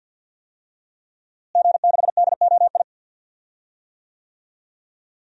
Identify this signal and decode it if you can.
Morse code, heard as a keyed tone, about 49 wpm, 700 Hz. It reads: G6DOI
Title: DXCC-100 Call Signs - 5 Characters - Repeated 3x - 50wpm